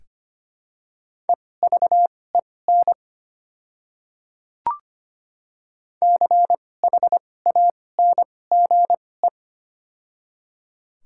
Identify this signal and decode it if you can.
Morse code, heard as a keyed tone, about 25 wpm, spaced as for 18 wpm, 700 Hz. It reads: EVEN CHANGE
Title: 2x Character Spacing - Top 500 Words - 25wpm